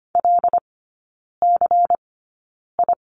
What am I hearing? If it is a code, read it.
Morse code, heard as a keyed tone, about 25 wpm, spaced as for 9 wpm, 700 Hz. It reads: LCI
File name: course / 25:10wpm / 093 Lesson 16e - Letter c - Three Characters 25wpm